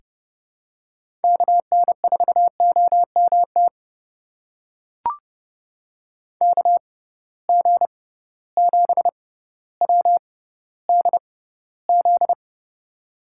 Morse code, heard as a keyed tone, about 30 wpm, spaced as for 15 wpm, 700 Hz. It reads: KN4OMT KG7WDZ
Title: USA Call Signs - 6 Characters - 6x Character Spacing - 30wpm